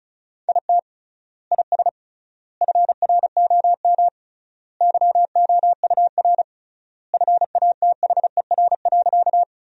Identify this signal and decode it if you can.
Morse code, heard as a keyed tone, about 36 wpm, 700 Hz. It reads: IT IS FROM YOUR FATHER.
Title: 3x Word-Spacing - Sentences from Top 300 Words - 35wpm